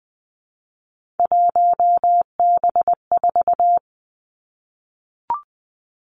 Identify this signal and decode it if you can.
Morse code, heard as a keyed tone, about 20 wpm, 700 Hz. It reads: 1B4
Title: Three Challenging Letter-Number Combinations 20wpm